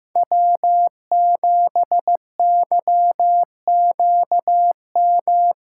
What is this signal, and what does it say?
Morse code, heard as a keyed tone, about 16 wpm, 700 Hz. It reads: W7YQM
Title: USA Callsign DE Callsign - 5 Characters - 15wpm